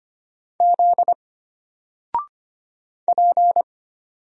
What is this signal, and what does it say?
Morse code, heard as a keyed tone, about 25 wpm, 700 Hz. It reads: Z P
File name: Single-Letter-Number-25wpm